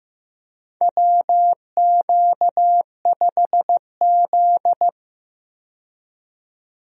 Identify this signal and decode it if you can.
Morse code, heard as a keyed tone, about 15 wpm, 700 Hz. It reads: WQ5Z
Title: USA Call Signs - 4 Characters - 15wpm